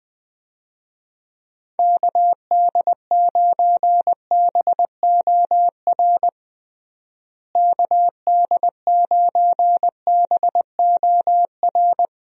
Morse code, heard as a keyed tone, about 20 wpm, 700 Hz. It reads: KD9BOR KD9BOR
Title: USA Call Signs - 6 Characters - Repeated 3x - 20wpm